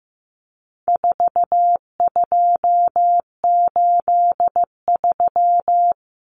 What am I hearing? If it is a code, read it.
Morse code, heard as a keyed tone, about 15 wpm, 700 Hz. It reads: 4283
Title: Common Cut Numbers - 4-Digit 15wpm